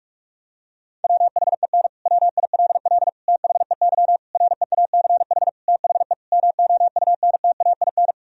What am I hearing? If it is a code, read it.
Morse code, heard as a keyed tone, about 46 wpm, 700 Hz. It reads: WHEN WILL THEY REACH THE MOUNTAIN
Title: Sentences from Top 1000 Words - Review Part 4 - 45wpm